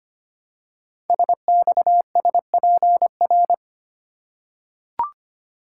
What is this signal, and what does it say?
Morse code, heard as a keyed tone, about 25 wpm, 700 Hz. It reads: SXSPR
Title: Code Groups - 5 Letters - 25wpm